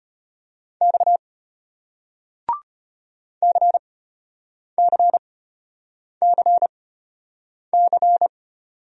Morse code, heard as a keyed tone, about 33 wpm, spaced as for 8 wpm, 700 Hz. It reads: X CCCC